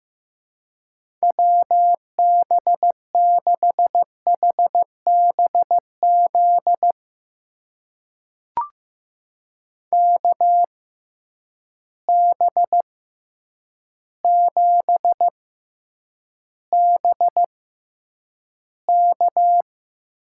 Morse code, heard as a keyed tone, about 15 wpm, spaced as for 8 wpm, 700 Hz. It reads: WB6HBZ KB7BK
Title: USA Call Signs - 6 Characters - 6x Character Spacing - 15wpm